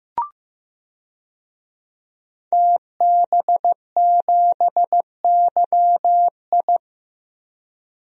Morse code, heard as a keyed tone, about 15 wpm, 700 Hz. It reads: TB7YI
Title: DXCC-100 Call Signs - 5 Characters - Repeated 2x - 15wpm